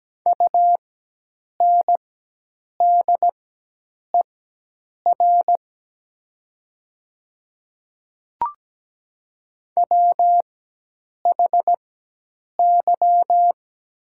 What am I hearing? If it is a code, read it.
Morse code, heard as a keyed tone, about 17 wpm, spaced as for 8 wpm, 700 Hz. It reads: UNDER WHY
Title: 4x Character Spacing - Top 200 Words - 17wpm